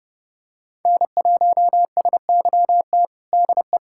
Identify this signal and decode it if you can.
Morse code, heard as a keyed tone, about 32 wpm, 700 Hz. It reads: N1SYT DE
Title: USA Callsign DE Callsign 30wpm